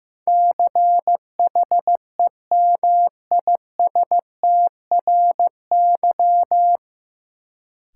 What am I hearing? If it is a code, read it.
Morse code, heard as a keyed tone, about 15 wpm, 700 Hz. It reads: CHEMISTRY
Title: Sentences from Top 1000 Words - Review Part 6 - New Word Drill - 15wpm